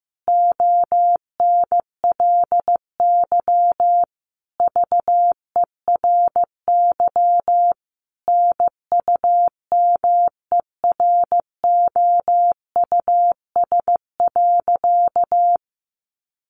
Morse code, heard as a keyed tone, about 15 wpm, 700 Hz. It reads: ONLY VERY NUMEROUS.
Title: Sentences from Top 1400 Words 15wpm